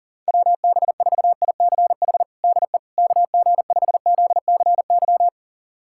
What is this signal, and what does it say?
Morse code, heard as a keyed tone, about 40 wpm, 700 Hz. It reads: WB4ICH DE KG5ZCY
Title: USA Callsign DE Callsign - 6 Characters - 40wpm